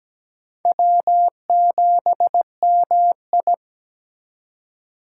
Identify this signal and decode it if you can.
Morse code, heard as a keyed tone, about 20 wpm, 700 Hz. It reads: W7MI
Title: Speed Racing - USA 1x2 Call Signs 17wpm